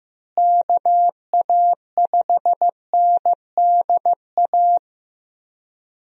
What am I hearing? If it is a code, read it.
Morse code, heard as a keyed tone, about 15 wpm, 700 Hz. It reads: KA5NDA
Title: USA Call Signs - 6 Characters - Repeated 3x - 15wpm